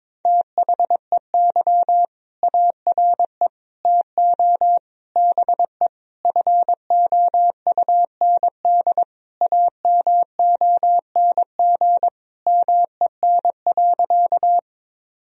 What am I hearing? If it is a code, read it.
Morse code, heard as a keyed tone, about 22 wpm, 690 Hz. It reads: THEY ARE TO BE FOUND AMONG MEN.
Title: Sentences from Top 1000 Words - Review Part 3 - 22wpm